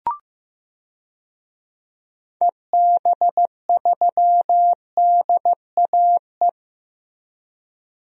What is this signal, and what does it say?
Morse code, heard as a keyed tone, about 15 wpm, 700 Hz. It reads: EB3DAE
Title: DXCC-100 Call Signs - 6 Characters - Repeated 3x - 15wpm